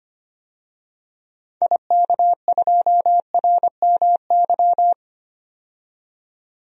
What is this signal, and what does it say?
Morse code, heard as a keyed tone, about 25 wpm, 700 Hz. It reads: IK2RMY
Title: DXCC-100 Call Signs - 6 Characters - 25wpm